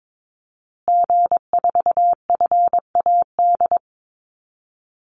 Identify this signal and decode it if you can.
Morse code, heard as a keyed tone, about 22 wpm, 700 Hz. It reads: G4FAD
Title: Call Signs Repeated 3x - 22wpm